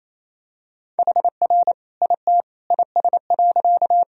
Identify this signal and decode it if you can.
Morse code, heard as a keyed tone, about 28 wpm, 700 Hz. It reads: HR IT IS.
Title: Sentences from Top 200 Words and Abbreviations 28wpm